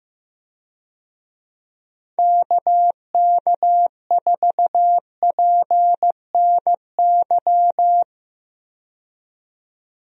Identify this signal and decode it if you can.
Morse code, heard as a keyed tone, about 15 wpm, 700 Hz. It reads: KK4PNY
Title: USA Call Signs - 6 Characters - 15wpm